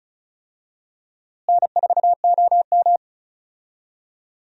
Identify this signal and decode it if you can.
Morse code, heard as a keyed tone, about 36 wpm, 700 Hz. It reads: N4OM
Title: USA Call Signs - 4 Characters - Repeated 2x - 35wpm